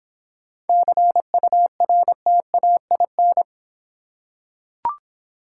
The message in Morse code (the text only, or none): CURTAIN